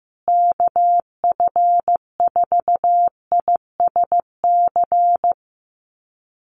Morse code, heard as a keyed tone, about 15 wpm, 700 Hz. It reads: KF4ISC
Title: USA Call Signs Repeated 3x 15wpm